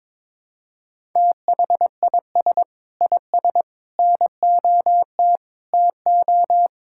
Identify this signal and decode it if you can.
Morse code, heard as a keyed tone, about 22 wpm, 700 Hz. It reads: THIS IS NOT TO
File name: practice / 22wpm / Sets of 6 Words - 22wpm